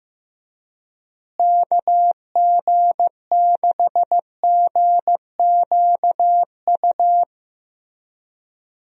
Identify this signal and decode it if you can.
Morse code, heard as a keyed tone, about 15 wpm, 700 Hz. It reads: KG6GQU
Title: USA Call Signs - 6 Characters - Repeated 3x - 15wpm